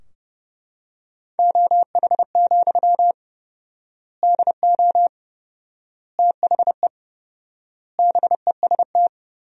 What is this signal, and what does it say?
Morse code, heard as a keyed tone, about 32 wpm, 700 Hz. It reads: OH, DO THE BEST
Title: 4x Word-Spacing - Sentences from Top 500 Words - 30wpm